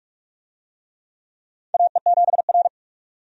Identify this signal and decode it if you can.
Morse code, heard as a keyed tone, about 48 wpm, 700 Hz. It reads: AE7R
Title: USA Call Signs - 4 Characters - 45wpm